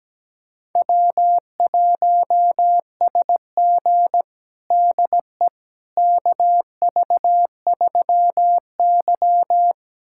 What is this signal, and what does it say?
Morse code, heard as a keyed tone, about 17 wpm, 700 Hz. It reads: W1SG DE KV3Y
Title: USA Callsign DE Callsign - 4 Characters - 17wpm